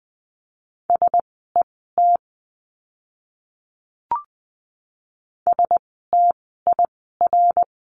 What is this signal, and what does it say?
Morse code, heard as a keyed tone, about 20 wpm, spaced as for 15 wpm, 700 Hz. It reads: SET STIR
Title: 068 Lesson 12f - Letter d - Words 20wpm